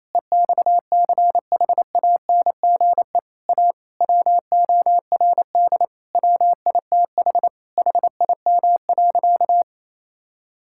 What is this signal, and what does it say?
Morse code, heard as a keyed tone, about 28 wpm, 700 Hz. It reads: EXCHANGE A WORD WITH HIM.
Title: Sentences from Top 1000 Words - Review Part 6 - 28wpm